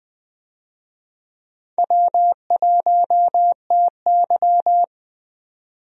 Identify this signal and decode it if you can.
Morse code, heard as a keyed tone, about 20 wpm, 700 Hz. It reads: W1TY
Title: USA Call Signs - 4 Characters - 20wpm